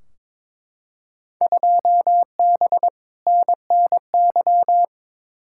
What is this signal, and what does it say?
Morse code, heard as a keyed tone, about 22 wpm, 700 Hz. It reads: 2B NNY
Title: Field Day Exchanges Encore - 22wpm